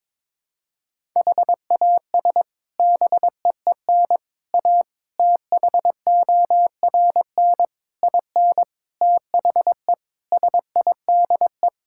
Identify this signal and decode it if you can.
Morse code, heard as a keyed tone, about 22 wpm, 700 Hz. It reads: HAS BEEN A THORN IN THE SIDE
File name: practice / 22wpm / Sets of 8 Words - 22wpm